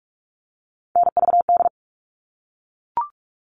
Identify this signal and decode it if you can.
Morse code, heard as a keyed tone, about 44 wpm, 700 Hz. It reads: NVD
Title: All Cut Numbers - 3-Digit 45wpm